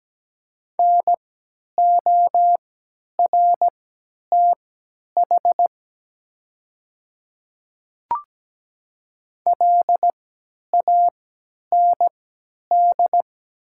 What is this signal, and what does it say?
Morse code, heard as a keyed tone, about 17 wpm, spaced as for 10 wpm, 700 Hz. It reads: NORTH LAND
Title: 3x Character Spacing - Top 500 Words - 17wpm